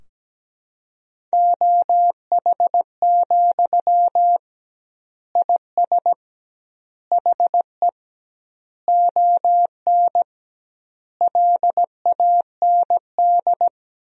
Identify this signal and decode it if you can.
Morse code, heard as a keyed tone, about 17 wpm, 700 Hz. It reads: OH, IS HE ON LAND
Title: 2x Word-Spacing - Sentences from Top 500 Words - 17wpm